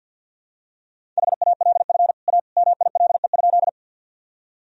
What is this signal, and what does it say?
Morse code, heard as a keyed tone, about 51 wpm, 700 Hz. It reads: HALF A MILE?
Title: Sentences from Top 1000 Words - Review Part 1 - 50wpm